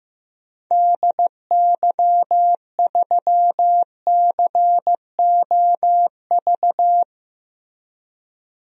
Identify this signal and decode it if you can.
Morse code, heard as a keyed tone, about 15 wpm, 700 Hz. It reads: DY3COV
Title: DXCC-100 Call Signs - 6 Characters - 15wpm